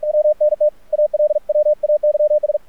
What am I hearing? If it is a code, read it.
Morse code, heard as a keyed tone, about 46 wpm, 600 Hz, with radio noise. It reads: WNT ALWAYS
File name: archived / CRSnet